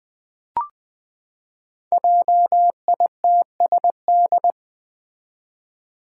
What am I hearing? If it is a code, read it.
Morse code, heard as a keyed tone, about 20 wpm, 700 Hz. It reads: JITSD
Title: Five Challenging Letter-Number Combinations 20wpm